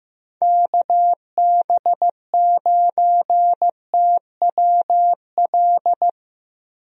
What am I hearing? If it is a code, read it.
Morse code, heard as a keyed tone, about 15 wpm, 700 Hz. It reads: KB9TWL
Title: USA Call Signs - 6 Characters - Repeated 2x - 15wpm